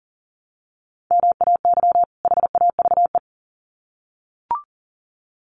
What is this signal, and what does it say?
Morse code, heard as a keyed tone, about 40 wpm, 700 Hz. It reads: MAY HAVE
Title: Sets of 2 Words 40wpm